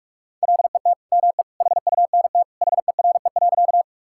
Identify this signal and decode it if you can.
Morse code, heard as a keyed tone, about 46 wpm, 700 Hz. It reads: LET ME HUNT HERE.
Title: Sentences from Top 1600 Words 45wpm